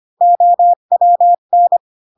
Morse code, heard as a keyed tone, about 25 wpm, 700 Hz. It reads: OWN